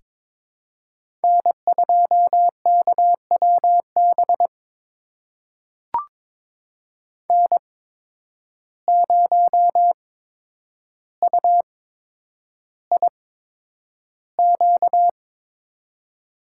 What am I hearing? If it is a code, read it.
Morse code, heard as a keyed tone, about 22 wpm, spaced as for 9 wpm, 700 Hz. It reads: N2KWB N0UIQ
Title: USA Call Signs - 5 Characters - 8x Character Spacing - 22wpm